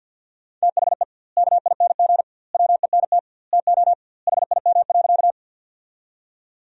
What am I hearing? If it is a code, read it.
Morse code, heard as a keyed tone, about 50 wpm, 700 Hz. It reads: THE KING WENT TO HIM.
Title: 2x Word-Spacing - Sentences from Top 400 Words - 50wpm